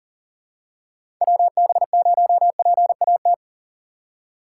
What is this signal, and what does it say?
Morse code, heard as a keyed tone, about 40 wpm, 700 Hz. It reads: WB0PAT
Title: USA Call Signs - 6 Characters - 3x Character Spacing - 40wpm